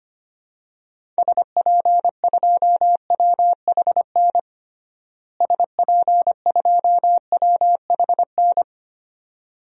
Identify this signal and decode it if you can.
Morse code, heard as a keyed tone, about 25 wpm, 700 Hz. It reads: SP2WHN SP2WHN
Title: DXCC-100 Call Signs - 6 Characters - Repeated 2x - 25wpm